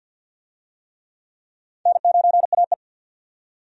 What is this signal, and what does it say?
Morse code, heard as a keyed tone, about 50 wpm, 700 Hz. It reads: N9AE
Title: USA Call Signs - 4 Characters - 50wpm